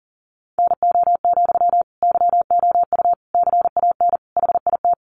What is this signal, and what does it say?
Morse code, heard as a keyed tone, about 40 wpm, 700 Hz. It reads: NO, YOU CAN HIT
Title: Sentences from Top 1200 Words 40wpm